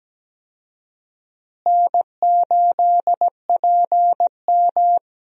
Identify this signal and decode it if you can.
Morse code, heard as a keyed tone, about 17 wpm, 700 Hz. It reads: N8PM
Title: USA Call Signs - 4 Characters - Repeated 2x - 17wpm